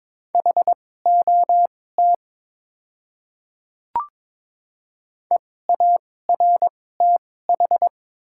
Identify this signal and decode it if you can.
Morse code, heard as a keyed tone, about 22 wpm, spaced as for 16 wpm, 700 Hz. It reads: HOT EARTH
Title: 2x Character Spacing - Top 200 Words - 22wpm